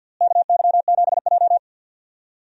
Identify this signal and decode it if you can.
Morse code, heard as a keyed tone, about 52 wpm, 700 Hz. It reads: KY7J
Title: USA Call Signs - 4 Characters - Repeated 3x - 50wpm